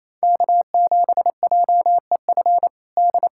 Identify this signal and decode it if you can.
Morse code, heard as a keyed tone, about 28 wpm, 700 Hz. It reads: K7JEF D
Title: USA Callsign DE Callsign - 5 Characters - 28wpm